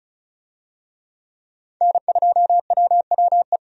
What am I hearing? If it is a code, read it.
Morse code, heard as a keyed tone, about 36 wpm, 700 Hz. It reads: N2WWE
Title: USA Call Signs - 5 Characters - 35wpm